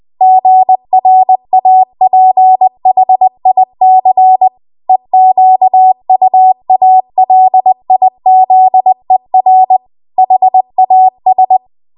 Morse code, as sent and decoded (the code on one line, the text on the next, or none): --. .-. .- .--. .... .. -.-. / . --.- ..- .- .-.. .. --.. .-. / .... .- ...
GRAPHIC EQUALIZER HAS